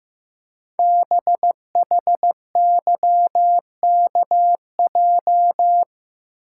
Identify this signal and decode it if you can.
Morse code, heard as a keyed tone, about 15 wpm, 700 Hz. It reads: BHYKJ
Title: Code Groups - 5 Letters - 15wpm